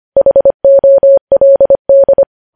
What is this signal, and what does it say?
Morse code, heard as a keyed tone, about 25 wpm, 550 Hz. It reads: HOLD
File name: podcast / News headlines-1x-25x25